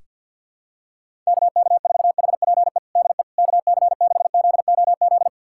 Morse code, heard as a keyed tone, about 46 wpm, 700 Hz. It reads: KK4HPE DE KC6ZOZ
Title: USA Callsign DE Callsign - 6 Characters - 50wpm